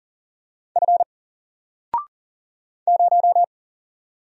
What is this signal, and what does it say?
Morse code, heard as a keyed tone, about 39 wpm, 700 Hz. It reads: F 0
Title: Single Letter-Number Alt 40wpm